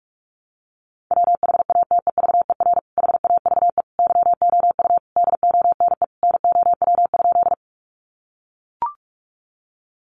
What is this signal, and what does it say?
Morse code, heard as a keyed tone, about 44 wpm, 700 Hz. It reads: WHATEVER HAVE YOU DONE NOW?
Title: Sentences from Top 1100 Words 45wpm